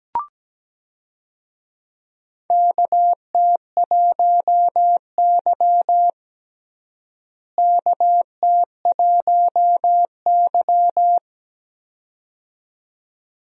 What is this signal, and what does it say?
Morse code, heard as a keyed tone, about 17 wpm, 700 Hz. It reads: KT1Y KT1Y
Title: USA Call Signs - 4 Characters - Repeated 2x - 17wpm